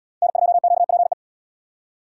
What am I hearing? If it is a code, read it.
Morse code, heard as a keyed tone, about 69 wpm, 700 Hz. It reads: N2QPE